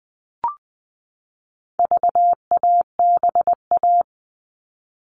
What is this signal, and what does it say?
Morse code, heard as a keyed tone, about 20 wpm, 700 Hz. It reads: VABA
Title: All Cut Numbers - 4-Digit 20wpm